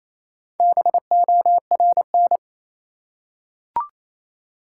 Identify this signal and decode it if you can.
Morse code, heard as a keyed tone, about 28 wpm, 700 Hz. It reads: BORN
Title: Four Letter Words - 28wpm